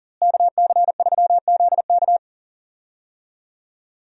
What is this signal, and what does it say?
Morse code, heard as a keyed tone, about 40 wpm, 700 Hz. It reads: KC3ZK
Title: USA Call Signs - 5 Characters - Repeated 2x - 40wpm